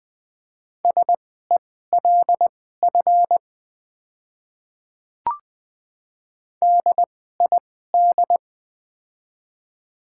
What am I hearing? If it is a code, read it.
Morse code, heard as a keyed tone, about 20 wpm, spaced as for 15 wpm, 700 Hz. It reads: SELF DID
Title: 2x Character Spacing - Top 200 Words - 20wpm